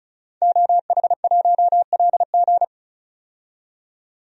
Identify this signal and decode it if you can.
Morse code, heard as a keyed tone, about 36 wpm, 700 Hz. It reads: OH1LG
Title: DXCC-100 Call Signs - 5 Characters - 35wpm